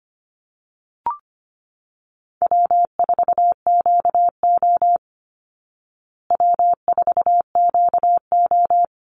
Morse code, heard as a keyed tone, about 25 wpm, 700 Hz. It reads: W4QO W4QO
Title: Call Signs Repeated 3x - 25wpm